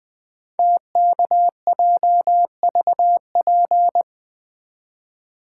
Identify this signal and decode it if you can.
Morse code, heard as a keyed tone, about 20 wpm, 700 Hz. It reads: TKJVP
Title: Code Groups - 5 Letters - 20wpm